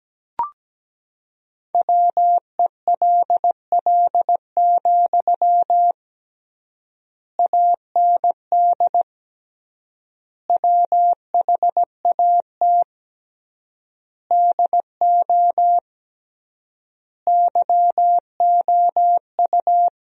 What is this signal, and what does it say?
Morse code, heard as a keyed tone, about 17 wpm, 700 Hz. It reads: WELL, AND WHAT DO YOU
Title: 3x Word-Spacing - Sentences from Top 200 Words - 17wpm